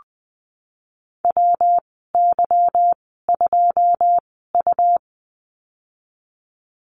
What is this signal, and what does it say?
Morse code, heard as a keyed tone, about 19 wpm, spaced as for 14 wpm, 700 Hz. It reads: WY2U
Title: 186 Lesson 30g - Letter b - Call Signs 20wpm